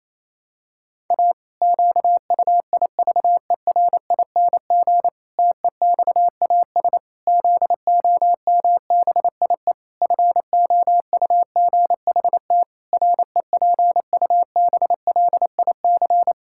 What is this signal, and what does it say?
Morse code, heard as a keyed tone, about 27 wpm, 700 Hz. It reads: A QUIVERING TEXAS ZOMBIE FOUGHT REPUBLIC
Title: Pangram Challenge v2 - 28wpm